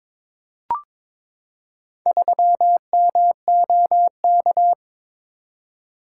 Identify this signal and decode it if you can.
Morse code, heard as a keyed tone, about 22 wpm, 700 Hz. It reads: 3MOK
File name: practice / 22wpm / Four Letter-Number Combinations - 22wpm